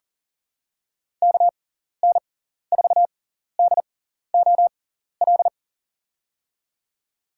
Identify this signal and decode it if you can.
Morse code, heard as a keyed tone, about 40 wpm, spaced as for 14 wpm, 700 Hz. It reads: KN4DOL
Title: USA Call Signs - 6 Characters - 6x Character Spacing - 40wpm